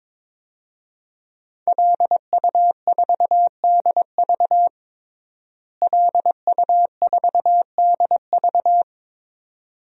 Morse code, heard as a keyed tone, about 22 wpm, 700 Hz. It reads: LU4DV LU4DV
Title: DXCC-100 Call Signs - 5 Characters - Repeated 2x - 22wpm